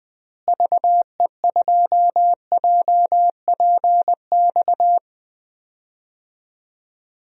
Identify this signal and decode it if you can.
Morse code, heard as a keyed tone, about 20 wpm, 700 Hz. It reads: VE2JPX
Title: DXCC-100 Call Signs - 6 Characters - Repeated 2x - 20wpm